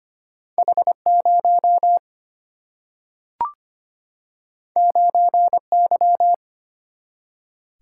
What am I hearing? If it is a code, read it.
Morse code, heard as a keyed tone, about 25 wpm, 700 Hz. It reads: H0 9Y
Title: Single-Letter Single-Number 25wpm